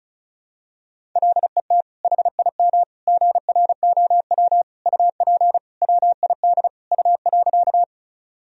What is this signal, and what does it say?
Morse code, heard as a keyed tone, about 36 wpm, 700 Hz. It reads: LET HIM GROW UP WID U.
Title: Sentences from Top 300 Words and Abbreviations 35wpm